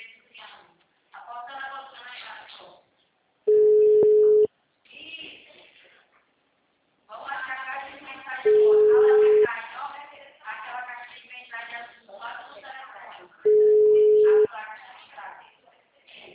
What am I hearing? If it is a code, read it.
Ringback tone (Europe, 425 Hz), with speech.